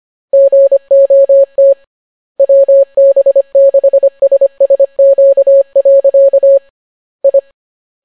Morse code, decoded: GOT WB6SSQ. I